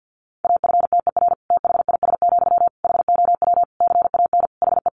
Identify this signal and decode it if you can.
Morse code, heard as a keyed tone, about 50 wpm, 700 Hz. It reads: AFTER THIS, HOW CAN HE
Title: Sentences from Top 1500 Words 50wpm